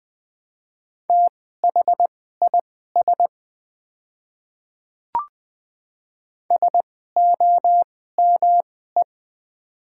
Morse code, heard as a keyed tone, about 20 wpm, spaced as for 14 wpm, 690 Hz. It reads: THIS SOME
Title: 2x Character Spacing - Top 100 Words - 20wpm